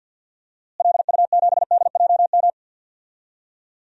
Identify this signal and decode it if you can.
Morse code, heard as a keyed tone, about 52 wpm, 700 Hz. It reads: LU7DJM
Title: DXCC-100 Call Signs - 6 Characters - Repeated 2x - 50wpm